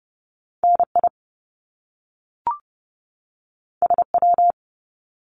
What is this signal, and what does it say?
Morse code, heard as a keyed tone, about 30 wpm, 700 Hz. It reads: NI SW
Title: Top Two Letter Combinations in Words 30wpm